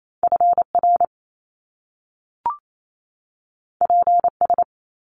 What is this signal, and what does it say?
Morse code, heard as a keyed tone, about 28 wpm, 700 Hz. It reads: FR PS
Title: Top Two Letter Combinations in Words 28wpm